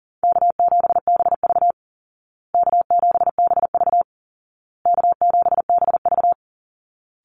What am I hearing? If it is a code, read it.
Morse code, heard as a keyed tone, about 40 wpm, 700 Hz. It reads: K7BV K7BV K7BV
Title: Easy Call Signs Repeated 3x - 40wpm